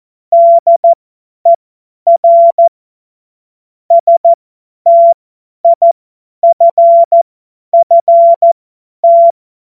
Morse code, heard as a keyed tone, about 14 wpm, spaced as for 10 wpm, 680 Hz. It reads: DER STIFFT
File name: Morsekurs / Witze / Witz 17.Oktober 2023 50 F70